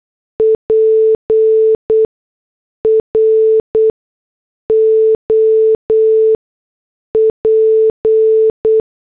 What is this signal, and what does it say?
Morse code, heard as a keyed tone, about 8 wpm, spaced as for 6 wpm, 440 Hz. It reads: PROP